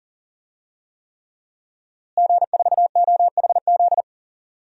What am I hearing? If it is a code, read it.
Morse code, heard as a keyed tone, about 40 wpm, 700 Hz. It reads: G4OHZ
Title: DXCC-100 Call Signs - 5 Characters - Repeated 3x - 40wpm